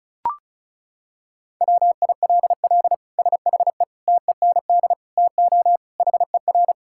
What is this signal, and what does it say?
Morse code, heard as a keyed tone, about 34 wpm, 700 Hz. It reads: WILL SHE TEND TO HER
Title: Sentences from Top 1000 Words - Review Part 5 - 35wpm